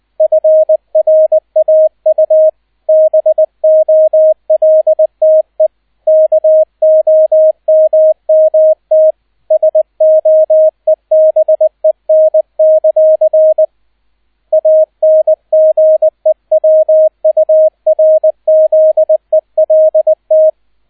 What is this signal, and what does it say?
Morse code, heard as a keyed tone, about 20 wpm, 630 Hz, with radio noise. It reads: FRAU BOLTE KOMMT SOEBEN; ANGEWURZELT
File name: 002 Zweiter Streich 20wpm